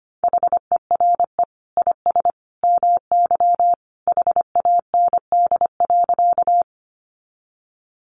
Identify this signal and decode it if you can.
Morse code, heard as a keyed tone, about 25 wpm, 700 Hz. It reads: HERE IS MY HAND.
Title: Sentences from Top 200 Words 25wpm